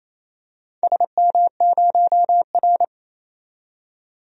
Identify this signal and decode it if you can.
Morse code, heard as a keyed tone, about 29 wpm, 700 Hz. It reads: SM0R
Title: DXCC-100 Call Signs - 4 Characters - 28wpm